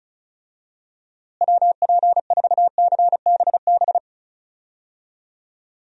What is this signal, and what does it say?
Morse code, heard as a keyed tone, about 34 wpm, 700 Hz. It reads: WP4CBB
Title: USA Call Signs - 6 Characters - Repeated 3x - 35wpm